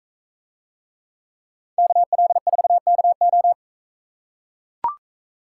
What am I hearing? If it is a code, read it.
Morse code, heard as a keyed tone, about 40 wpm, 700 Hz. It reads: KL4KO